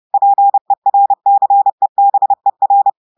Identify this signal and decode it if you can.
Morse code, heard as a keyed tone, about 30 wpm, 800 Hz. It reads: PERCEBER
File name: portugais texte en clair 2 vitesse 30 mots minutes